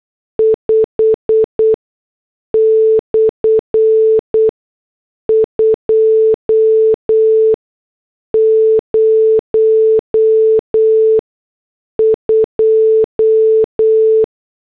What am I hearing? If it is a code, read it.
Morse code, heard as a keyed tone, about 8 wpm, spaced as for 6 wpm, 440 Hz. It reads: 5/202